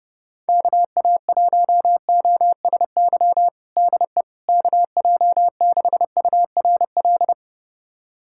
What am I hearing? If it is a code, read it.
Morse code, heard as a keyed tone, about 30 wpm, 690 Hz. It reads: KA1OSY DE KJ6URL
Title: USA Callsign DE Callsign - 6 Characters - 30wpm